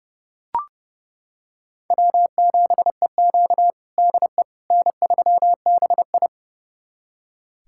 Morse code, heard as a keyed tone, about 30 wpm, 700 Hz. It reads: W7EQ DE N3BI
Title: USA Callsign DE Callsign - 4 Characters - 30wpm